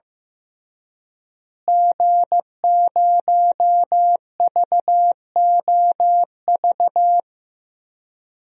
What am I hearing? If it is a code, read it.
Morse code, heard as a keyed tone, about 15 wpm, 700 Hz. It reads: G0VOV
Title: DXCC-100 Call Signs - 5 Characters - Repeated 2x - 15wpm